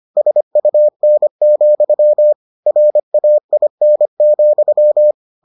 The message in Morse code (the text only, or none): SUN, RAIN,